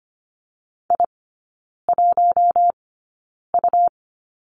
Morse code, heard as a keyed tone, about 25 wpm, spaced as for 9 wpm, 700 Hz. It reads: I1U
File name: Lesson 17e - Letter u - Three Characters 25wpm